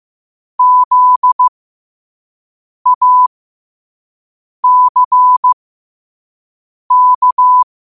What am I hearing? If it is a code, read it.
Morse code, heard as a keyed tone, about 15 wpm, spaced as for 5 wpm, 1000 Hz. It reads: ZACK
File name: qotdfiles / 5wpm